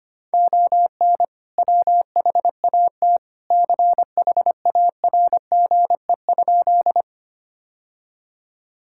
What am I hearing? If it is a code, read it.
Morse code, heard as a keyed tone, about 25 wpm, 700 Hz. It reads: ON WHAT CHARGE?
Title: Sentences from Top 1000 Words - Review Part 6 - 25wpm